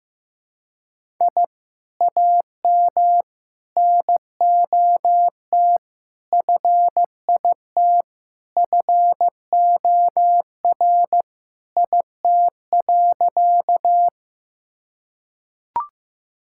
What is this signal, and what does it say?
Morse code, heard as a keyed tone, about 15 wpm, 700 Hz. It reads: I AM NOT FIT FOR IT.
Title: Sentences from Top 1000 Words - Review Part 5 - 15wpm